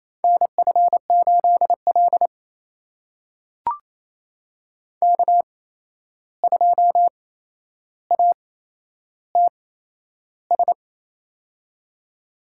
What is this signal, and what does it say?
Morse code, heard as a keyed tone, about 28 wpm, spaced as for 11 wpm, 700 Hz. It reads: NF8L K2ATS
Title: USA Call Signs - 8x Character Spacing - 28wpm